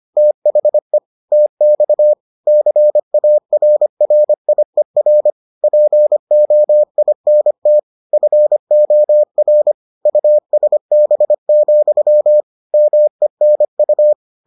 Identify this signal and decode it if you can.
Morse code, heard as a keyed tone, about 25 wpm, 600 Hz. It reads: THE TX CARRIER POINT FOR USB, MENU